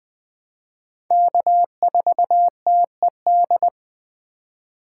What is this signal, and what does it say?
Morse code, heard as a keyed tone, about 20 wpm, 700 Hz. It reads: K4TED
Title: Speed Racing - USA 1x3 Call Signs 17wpm